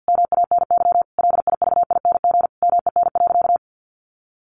Morse code, heard as a keyed tone, about 50 wpm, 700 Hz. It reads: MANY LIVING MEN.
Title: Sentences from Top 1300 Words 50wpm